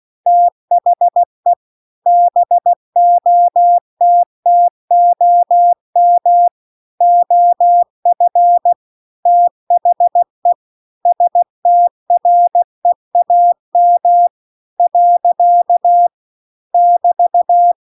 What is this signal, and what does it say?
Morse code, heard as a keyed tone, about 16 wpm, 700 Hz. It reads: THE BOTTOM OF THE STREAM . <BT>